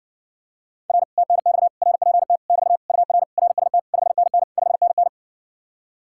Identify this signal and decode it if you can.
Morse code, heard as a keyed tone, about 60 wpm, 700 Hz. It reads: R TNX RPT <BT> UR RST 5NN 5NN